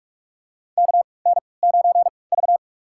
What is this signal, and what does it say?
Morse code, heard as a keyed tone, about 44 wpm, spaced as for 25 wpm, 690 Hz. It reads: KN9V
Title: USA Call Signs - 4 Characters - 3x Character Spacing - 45wpm